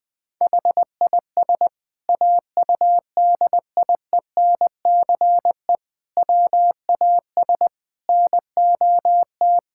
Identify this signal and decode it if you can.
Morse code, heard as a keyed tone, about 20 wpm, 700 Hz. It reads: HIS AUDIENCE WAS NOT
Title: Sentences from Top 1000 Words - Review Part 5 - 20wpm